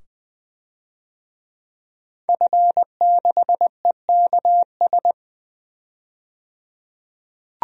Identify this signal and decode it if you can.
Morse code, heard as a keyed tone, about 20 wpm, 700 Hz. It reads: F6EKS
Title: DXCC-100 Call Signs - 5 Characters - 20wpm